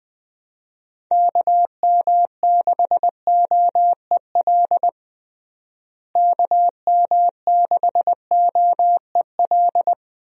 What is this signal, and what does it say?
Morse code, heard as a keyed tone, about 20 wpm, 700 Hz. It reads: KM6OEL KM6OEL
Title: USA Call Signs - 6 Characters - Repeated 3x - 20wpm